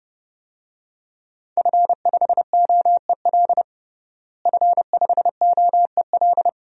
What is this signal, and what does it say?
Morse code, heard as a keyed tone, about 30 wpm, 700 Hz. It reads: F5OEL F5OEL
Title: DXCC-100 Call Signs - 5 Characters - Repeated 2x - 30wpm